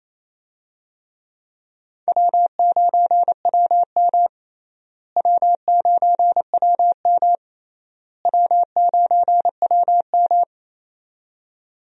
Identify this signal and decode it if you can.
Morse code, heard as a keyed tone, about 28 wpm, 700 Hz. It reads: W9WM W9WM W9WM
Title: USA Call Signs - 4 Characters - Repeated 3x - 28wpm